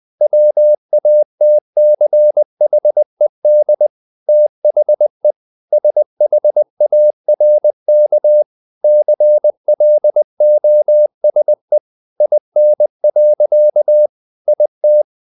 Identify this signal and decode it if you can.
Morse code, heard as a keyed tone, about 20 wpm, 600 Hz. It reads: WATCHED THE SHARK CLOSE IN. IT
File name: the old man and the sea part 2 20wpm